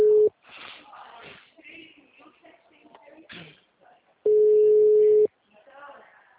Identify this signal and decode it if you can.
Ringback tone (Europe, 425 Hz), with speech.